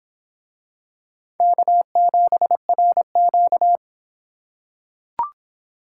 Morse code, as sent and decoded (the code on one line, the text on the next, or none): -.- --... .-. --.-
K7RQ